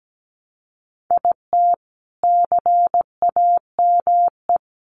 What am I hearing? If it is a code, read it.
Morse code, heard as a keyed tone, about 17 wpm, 700 Hz. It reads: IT CAME
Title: Sentences from Top 1300 Words 17wpm